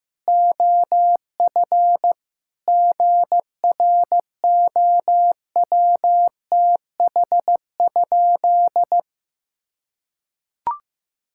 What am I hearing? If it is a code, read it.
Morse code, heard as a keyed tone, about 15 wpm, 700 Hz. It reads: OF GROWTH?
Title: Sentences from Top 1000 Words - Review Part 5 - 15wpm